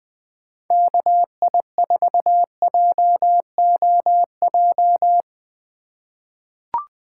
Morse code, heard as a keyed tone, about 20 wpm, 700 Hz. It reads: KI4JOJ
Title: USA Call Signs - 6 Characters - 2x Character Spacing - 20wpm